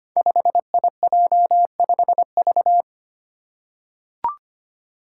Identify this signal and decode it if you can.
Morse code, heard as a keyed tone, about 25 wpm, 700 Hz. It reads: 5IJ5V
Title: Five Challenging Letter-Number Combinations 25wpm